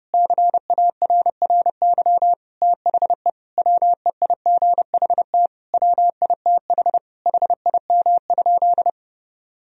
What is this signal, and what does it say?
Morse code, heard as a keyed tone, about 30 wpm, 700 Hz. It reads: CARRY THE WEIGHT WITH HIM?
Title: Sentences from Top 1000 Words - Review Part 3 - 30wpm